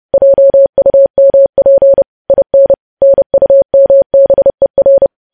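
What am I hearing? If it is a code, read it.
Morse code, heard as a keyed tone, about 30 wpm, 550 Hz. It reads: JUMP IN NUMBER
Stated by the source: News headlines sent in Morse at 30 wpm, preceded by VVV.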